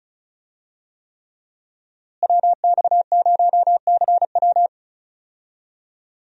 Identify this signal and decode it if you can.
Morse code, heard as a keyed tone, about 36 wpm, 700 Hz. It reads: WX0CW
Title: USA Call Signs - 5 Characters - Repeated 3x - 35wpm